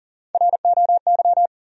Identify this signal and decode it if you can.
Morse code, heard as a keyed tone, about 37 wpm, 700 Hz. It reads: ROY
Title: USA Common Names - Max 4 Letters - 40wpm